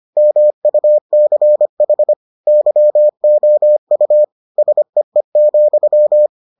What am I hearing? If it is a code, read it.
Morse code, heard as a keyed tone, about 25 wpm, 600 Hz. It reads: MUCH YOU SEE,